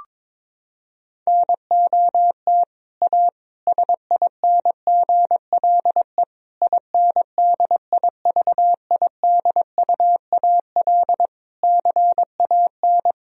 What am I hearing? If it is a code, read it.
Morse code, heard as a keyed tone, about 21 wpm, 700 Hz. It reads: NOT A SINGLE INDIVIDUAL CAN
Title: Sentences from Top 1000 Words - Review Part 4 - 22wpm